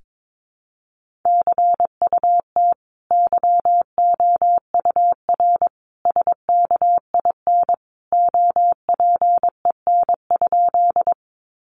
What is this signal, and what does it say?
Morse code, heard as a keyed tone, about 22 wpm, 700 Hz. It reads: CUT YOUR SKIN OPEN?
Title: Sentences from Top 1300 Words 22wpm